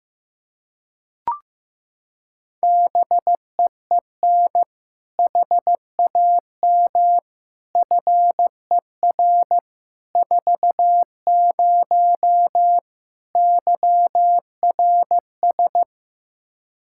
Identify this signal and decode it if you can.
Morse code, heard as a keyed tone, about 15 wpm, 700 Hz. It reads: BEEN HAM FER 40 YRS
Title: Practice QSO - Exchange 3 - More Info - 15wpm